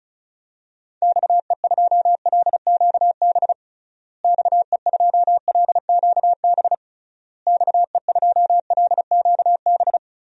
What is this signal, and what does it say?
Morse code, heard as a keyed tone, about 36 wpm, 700 Hz. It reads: XE2LQB XE2LQB XE2LQB
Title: DXCC-100 Call Signs - 6 Characters - Repeated 3x - 35wpm